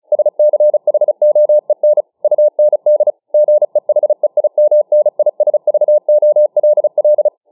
Morse code, heard as a keyed tone, about 34 wpm, 600 Hz. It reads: SCHOEN UND GEHEIMNISVOLL